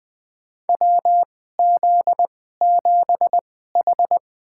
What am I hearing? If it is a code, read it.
Morse code, heard as a keyed tone, about 20 wpm, spaced as for 14 wpm, 700 Hz. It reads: WZ7H
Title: USA Call Signs - 4 Characters - 2x Character Spacing - 20wpm